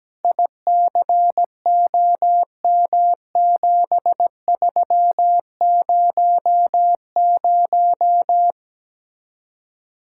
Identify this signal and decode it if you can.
Morse code, heard as a keyed tone, about 17 wpm, 700 Hz. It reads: ICOM7300
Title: Speed-Racing Common QSO Phrases - 17wpm